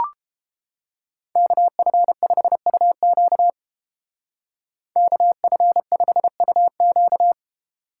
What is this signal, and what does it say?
Morse code, heard as a keyed tone, about 30 wpm, 700 Hz. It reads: KF5UQ KF5UQ